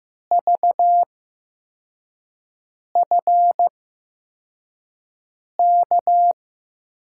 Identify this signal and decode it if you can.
Morse code, heard as a keyed tone, about 15 wpm, spaced as for 4 wpm, 700 Hz. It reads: VFK
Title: USA Call Signs - 6 Characters - 8x Character Spacing - 15wpm